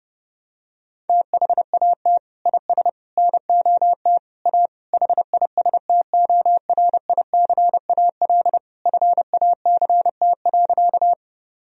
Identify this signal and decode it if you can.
Morse code, heard as a keyed tone, about 30 wpm, 700 Hz. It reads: THAT IS NOT A HISTORICAL FACT.
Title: Sentences from Top 1000 Words - Review Part 4 - 30wpm